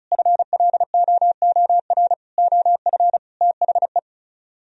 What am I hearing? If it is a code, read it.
Morse code, heard as a keyed tone, about 43 wpm, 700 Hz. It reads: FLOOR OF THE
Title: Speed-Racing - Sets of 5 Words 35wpm